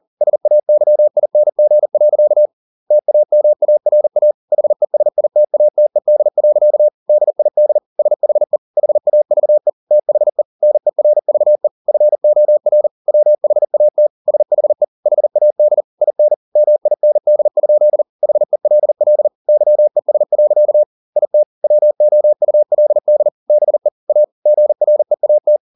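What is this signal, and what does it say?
Morse code, heard as a keyed tone, about 40 wpm, 600 Hz. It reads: SAYING. TAMARA HESITATED. DID SHE HAVE THE NERVE FOR WHAT SHE HAD IN MIND? HELL YES. IT WOULD BE A GREAT
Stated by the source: Morse code at 40 wpm, practice text, CW med Gnister.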